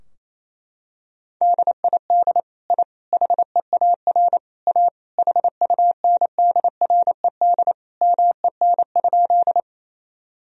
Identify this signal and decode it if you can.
Morse code, heard as a keyed tone, about 28 wpm, 700 Hz. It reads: DID I HEAR A HUNDRED MEN?
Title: Sentences from Top 1000 Words - Review Part 2 - 28wpm